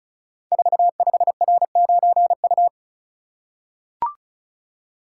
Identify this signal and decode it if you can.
Morse code, heard as a keyed tone, about 36 wpm, 700 Hz. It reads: 45R9U
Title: Code Groups - 5 Letter:Numbers - 35wpm